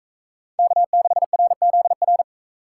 Morse code, heard as a keyed tone, about 40 wpm, 700 Hz. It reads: K6RZR